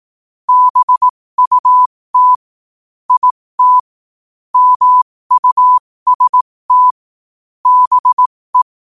Morse code, heard as a keyed tone, about 18 wpm, spaced as for 15 wpm, 1000 Hz. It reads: BUT IT MUST BE
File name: qotdfiles / 15wpm